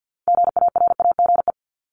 Morse code, heard as a keyed tone, about 50 wpm, 690 Hz. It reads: GARAGE
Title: Multi-letter Phonemes 50wpm